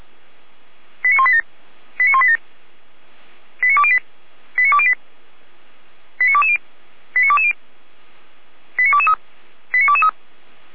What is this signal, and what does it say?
Five-tone selective call (ZVEI), heard as quick runs of five tones, tones of about 70 ms. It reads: 89187 89187 89208 89208 89220 89220 89223 89223
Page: /455/SDR/ZVEI/